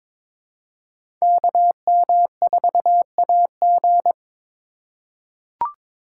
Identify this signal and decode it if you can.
Morse code, heard as a keyed tone, about 22 wpm, 700 Hz. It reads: KM4AG
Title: USA Call Signs - 4x Character Spacing - 22wpm